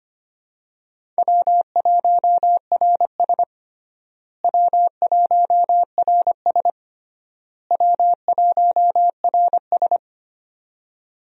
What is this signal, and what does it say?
Morse code, heard as a keyed tone, about 25 wpm, 700 Hz. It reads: W1RS W1RS W1RS
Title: USA Call Signs - 4 Characters - Repeated 3x - 25wpm